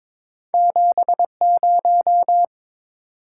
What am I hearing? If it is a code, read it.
Morse code, heard as a keyed tone, about 22 wpm, 700 Hz. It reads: 70
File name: practice / 22wpm / Two Letter-Number Combinations - 22wpm